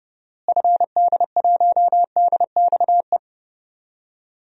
Morse code, heard as a keyed tone, about 30 wpm, 700 Hz. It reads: FD1DXE
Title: DXCC-100 Call Signs - 6 Characters - Repeated 3x - 30wpm